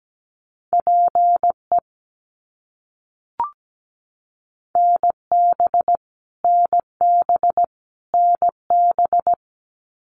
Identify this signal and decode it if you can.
Morse code, heard as a keyed tone, about 17 wpm, 700 Hz. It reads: PE NB NB NB
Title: Canadian Provinces and Territories - Repeated 3x - 17wpm